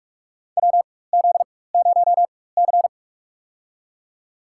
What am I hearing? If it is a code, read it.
Morse code, heard as a keyed tone, about 46 wpm, spaced as for 21 wpm, 700 Hz. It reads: WZ0C
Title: USA Call Signs - 4 Characters - 4x Character Spacing - 45wpm